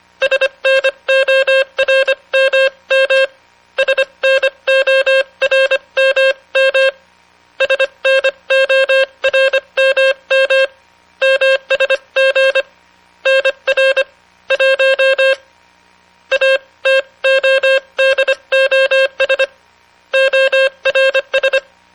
Morse code, heard as a keyed tone, about 25 wpm, spaced as for 22 wpm, 530 Hz, with noise. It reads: SNORMM SNORMM SNORMM MSG NR 1 ATODOS ORS